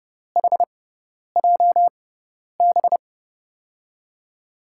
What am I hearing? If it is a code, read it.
Morse code, heard as a keyed tone, about 30 wpm, spaced as for 10 wpm, 700 Hz. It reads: HJB
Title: USA Call Signs - 6 Characters - 6x Character Spacing - 30wpm